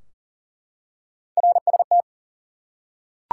Morse code, heard as a keyed tone, about 40 wpm, 700 Hz. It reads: RST
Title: Top 90 QSO Elements 40wpm